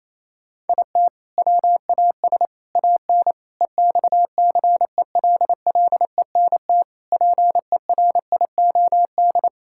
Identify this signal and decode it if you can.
Morse code, heard as a keyed tone, about 27 wpm, 700 Hz. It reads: IT WAS AN EXCELLENT PERIOD
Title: Sentences from Top 1000 Words - Review Part 5 - 28wpm